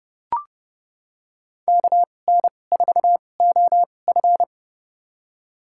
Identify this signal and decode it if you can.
Morse code, heard as a keyed tone, about 30 wpm, spaced as for 22 wpm, 700 Hz. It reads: KN4OF
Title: USA Call Signs - 5 Characters - 2x Character Spacing - 30wpm